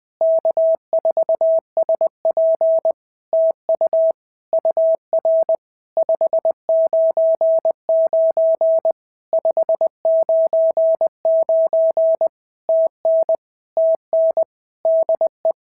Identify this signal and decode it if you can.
Morse code, heard as a keyed tone, about 20 wpm, 650 Hz. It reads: K4SP TU UR 599 599 TN TN DE
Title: POTA QSO Practice - 20wpm